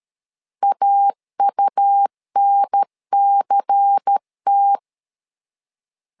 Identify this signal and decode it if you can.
Morse code, heard as a keyed tone, about 13 wpm, 800 Hz. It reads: AUNCT
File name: LEZIONI / 5 LBVC / 5 LBVC 12wpm 1955